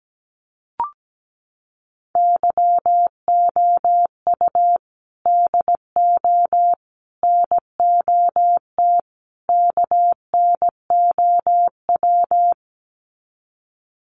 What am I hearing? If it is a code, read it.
Morse code, heard as a keyed tone, about 17 wpm, 700 Hz. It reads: YOU DO NOT KNOW
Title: Sets of 4 Words 17wpm